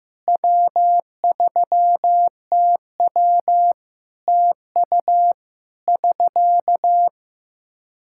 Morse code, heard as a keyed tone, about 15 wpm, 700 Hz. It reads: W3TW TU <SK>
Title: Practice QSO - Exchange 4 - Ending - 15wpm